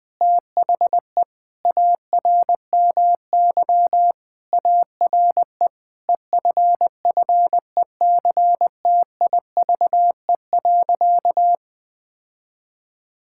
Morse code, heard as a keyed tone, about 20 wpm, 700 Hz. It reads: THE ARMY ARE EFFECTIVE.
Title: Sentences from Top 1000 Words - Review Part 6 - 20wpm